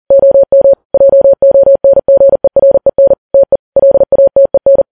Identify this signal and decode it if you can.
Morse code, heard as a keyed tone, about 40 wpm, 550 Hz. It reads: OM JONGEREN TE LATEN
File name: Files Web tls / Morse on9fv / morse40kps